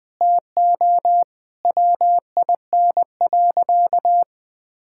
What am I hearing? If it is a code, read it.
Morse code, heard as a keyed tone, about 21 wpm, 700 Hz. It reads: TO WIN.
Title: Sentences with 3-Word Phrases - Spoken Beginning - 20wpm